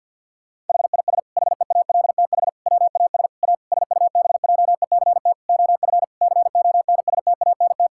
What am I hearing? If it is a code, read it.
Morse code, heard as a keyed tone, about 50 wpm, 700 Hz. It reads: HIS HEALTH WAS A SUBJECT OF CONSTANT
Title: Sentences from Top 1600 Words 50wpm